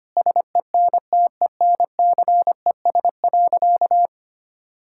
Morse code, heard as a keyed tone, about 25 wpm, 700 Hz. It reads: SENTENCES.
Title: Pangram Challenge v2 - 25wpm